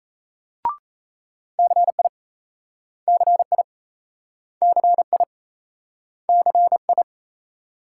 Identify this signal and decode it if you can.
Morse code, heard as a keyed tone, about 38 wpm, 700 Hz. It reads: CI CI CI CI